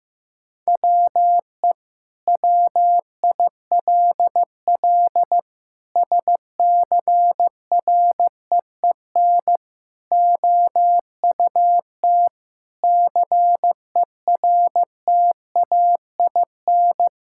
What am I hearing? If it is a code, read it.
Morse code, heard as a keyed tone, about 15 wpm, 700 Hz. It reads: WE WILL SCREEN OUT CERTAIN
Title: Sentences from Top 1000 Words - Review Part 6 - 15wpm